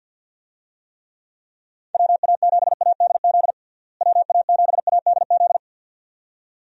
Morse code, heard as a keyed tone, about 52 wpm, 700 Hz. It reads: WA7ADZ WA7ADZ
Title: USA Call Signs - 6 Characters - Repeated 2x - 50wpm